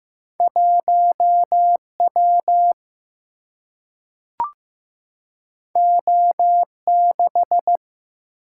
Morse code, heard as a keyed tone, about 15 wpm, 700 Hz. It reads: 1W O6
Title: Single-Letter Single-Number 15wpm